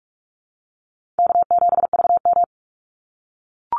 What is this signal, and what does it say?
Morse code, heard as a keyed tone, about 44 wpm, 700 Hz. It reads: K7VM
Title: Easy Call Signs - 45wpm